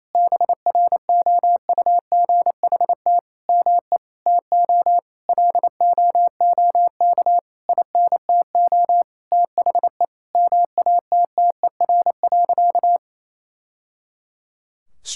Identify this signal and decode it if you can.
Morse code, heard as a keyed tone, about 28 wpm, 700 Hz. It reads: BROUGHT ME TO LOOK INTO THE MATTER.
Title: Sentences from Top 1000 Words - Review Part 6 - 28wpm